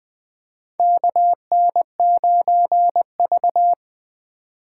Morse code, heard as a keyed tone, about 20 wpm, 700 Hz. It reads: KN9V
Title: USA Call Signs - 4 Characters - 3x Character Spacing - 20wpm